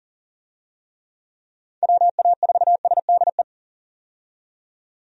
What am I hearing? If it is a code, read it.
Morse code, heard as a keyed tone, about 40 wpm, 700 Hz. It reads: WA4SDE
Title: USA Call Signs - 6 Characters - 40wpm